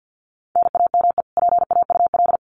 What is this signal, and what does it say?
Morse code, heard as a keyed tone, about 50 wpm, 700 Hz. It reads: NAME PAUL
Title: Common QSO Phrases 50wpm